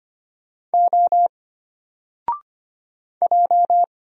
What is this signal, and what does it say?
Morse code, heard as a keyed tone, about 25 wpm, 700 Hz. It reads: O J
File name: Single-Letter-Number-25wpm